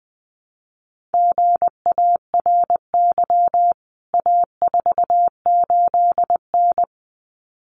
Morse code, heard as a keyed tone, about 20 wpm, 700 Hz. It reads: GARY A48N
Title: CWT Exchange 20wpm